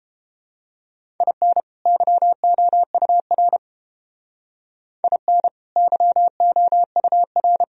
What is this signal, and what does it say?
Morse code, heard as a keyed tone, about 29 wpm, 700 Hz. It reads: IN YOUR IN YOUR